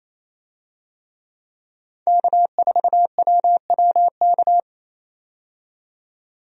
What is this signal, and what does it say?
Morse code, heard as a keyed tone, about 28 wpm, 700 Hz. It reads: K4WWK
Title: USA Call Signs - 5 Characters - 28wpm